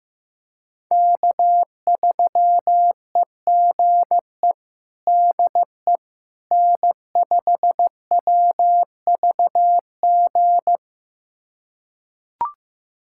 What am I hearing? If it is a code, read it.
Morse code, heard as a keyed tone, about 15 wpm, 700 Hz. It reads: K3EGE DE N5WVG
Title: USA Callsign DE Callsign - 5 Characters - 15wpm